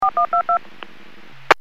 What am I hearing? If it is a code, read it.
DTMF, heard as telephone touch tones, tones of about 110 ms. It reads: 4122
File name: WPS / 01753 18-13 WPS 451.25 WPS